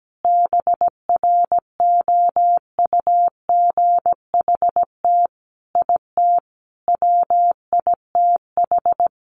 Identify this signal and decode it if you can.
Morse code, heard as a keyed tone, about 17 wpm, 700 Hz. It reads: BROUGHT IT WITH
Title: Sentences from Top 500 Words 17wpm